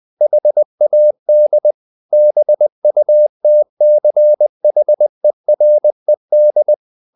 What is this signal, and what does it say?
Morse code, heard as a keyed tone, about 20 wpm, 600 Hz. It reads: HAD BUTCHERED
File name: the old man and the sea / the old man and the sea part 1 20wpm